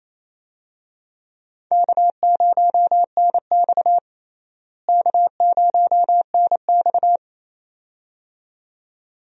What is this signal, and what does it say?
Morse code, heard as a keyed tone, about 28 wpm, 700 Hz. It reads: K0NX K0NX
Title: USA Call Signs - 4 Characters - Repeated 2x - 28wpm